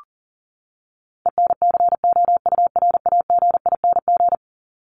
Morse code, heard as a keyed tone, about 40 wpm, 700 Hz. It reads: ENCOURAGING
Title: Top 1751-2000 Words 40wpm